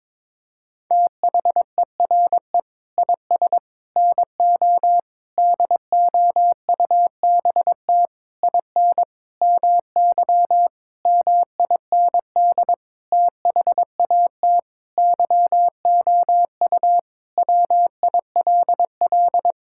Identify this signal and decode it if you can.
Morse code, heard as a keyed tone, about 22 wpm, 700 Hz. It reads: THERE IS NO DOUBT IN MY MIND THAT YOU WILL
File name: practice / 22wpm / Sentences with 8-Word Phrases - 22wpm